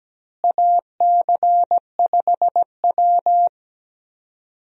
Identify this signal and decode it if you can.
Morse code, heard as a keyed tone, about 17 wpm, 700 Hz. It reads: AC5W
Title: USA Call Signs - 4 Characters - Repeated 2x - 17wpm